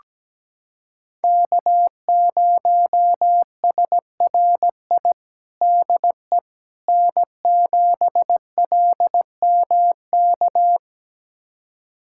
Morse code, heard as a keyed tone, about 17 wpm, 700 Hz. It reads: K0SRI DE N7LMK
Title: USA Callsign DE Callsign - 5 Characters - 17wpm